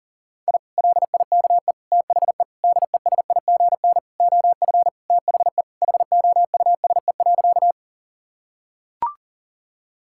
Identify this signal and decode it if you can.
Morse code, heard as a keyed tone, about 37 wpm, 700 Hz. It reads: I LIKE THE DESIGN OF THE HOUSE.
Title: Sentences from Top 1000 Words - Review Part 5 - 40wpm